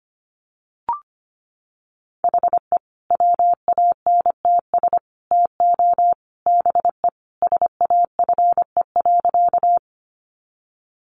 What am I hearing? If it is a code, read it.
Morse code, heard as a keyed tone, about 25 wpm, 700 Hz. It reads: HE WANTS TO BE SAFE.
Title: Additional Easy Phrases 25wpm